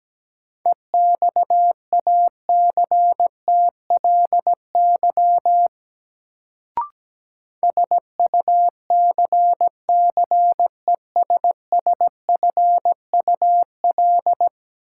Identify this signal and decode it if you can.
Morse code, heard as a keyed tone, about 17 wpm, 700 Hz. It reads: EXACTLY SUCCESSFUL
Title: Sentences from Top 900 Words - New Word Drill - 17wpm